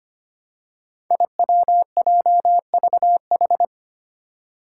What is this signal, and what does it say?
Morse code, heard as a keyed tone, about 25 wpm, 700 Hz. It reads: IWJVH
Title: Five Challenging Letter-Number Combinations 25wpm